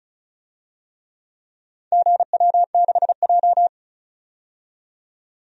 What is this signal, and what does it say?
Morse code, heard as a keyed tone, about 34 wpm, 700 Hz. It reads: GW6J
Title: DXCC-100 Call Signs - 4 Characters - 35wpm